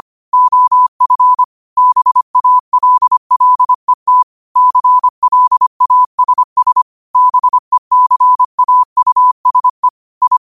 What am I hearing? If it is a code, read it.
Morse code, heard as a keyed tone, about 25 wpm, 1000 Hz. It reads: OF BALLET CLASS BECAUSE I
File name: qotdfiles / 25wpm